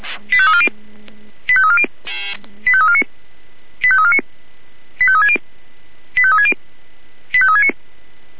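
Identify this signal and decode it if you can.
Five-tone selective call (ZVEI), heard as quick runs of five tones, tones of about 70 ms. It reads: 85340 85340 85349 85349 85360 85360 85369
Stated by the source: ZVEI five-tone call 85202 received by SDR.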